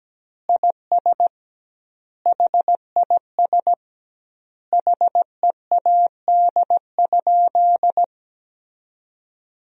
Morse code, heard as a keyed tone, about 17 wpm, 700 Hz. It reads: IS HIS HEAD?
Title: 2x Word-Spacing - Sentences from Top 300 Words - 17wpm